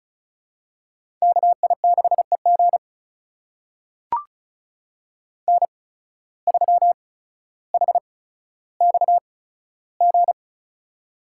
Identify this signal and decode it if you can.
Morse code, heard as a keyed tone, about 36 wpm, spaced as for 15 wpm, 700 Hz. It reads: KI6EG N3HXG
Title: USA Call Signs - 5 Characters - 8x Character Spacing - 35wpm